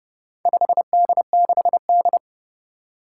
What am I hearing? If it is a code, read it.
Morse code, heard as a keyed tone, about 30 wpm, 700 Hz. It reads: 5D6D
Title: Four Challenging Letter-Number Combinations 30wpm